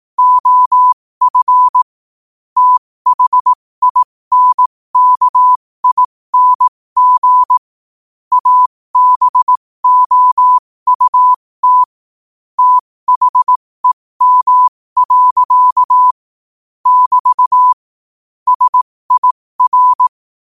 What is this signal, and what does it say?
Morse code, heard as a keyed tone, about 18 wpm, spaced as for 15 wpm, 1000 Hz. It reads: OF THINKING ABOUT THEM. <BT> SIR
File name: qotdfiles / 15wpm